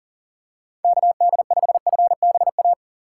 Morse code, heard as a keyed tone, about 40 wpm, 700 Hz. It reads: KD5FBA
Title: K1USN SST QSO Practice - 40wpm